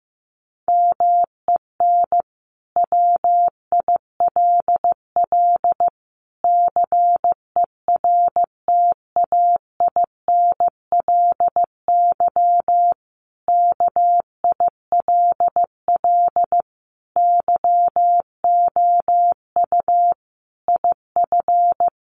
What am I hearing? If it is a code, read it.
Morse code, heard as a keyed tone, about 15 wpm, 700 Hz. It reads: MEN WILL CERTAINLY KILL YOU IF
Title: Sentences from Top 1500 Words 15wpm